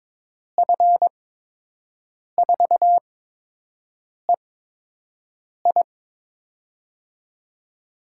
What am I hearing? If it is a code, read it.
Morse code, heard as a keyed tone, about 22 wpm, spaced as for 6 wpm, 700 Hz. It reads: F4EI
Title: USA Call Signs - 5 Characters - 8x Character Spacing - 22wpm